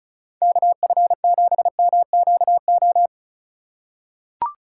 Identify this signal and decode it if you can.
Morse code, heard as a keyed tone, about 36 wpm, 700 Hz. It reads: KF7MQO
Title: USA Call Signs - 6 Characters - 2x Character Spacing - 35wpm